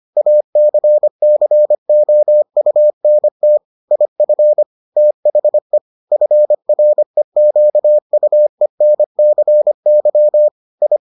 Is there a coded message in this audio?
Yes: Morse code from 0.2 to 11.0 s, heard as a keyed tone, about 25 wpm, 600 Hz. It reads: ACCOUNT IF THE FREQUENCY I